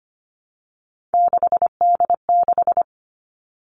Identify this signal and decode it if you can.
Morse code, heard as a keyed tone, about 25 wpm, 700 Hz. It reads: 6D6
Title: Three Challenging Letter-Number Combinations 25wpm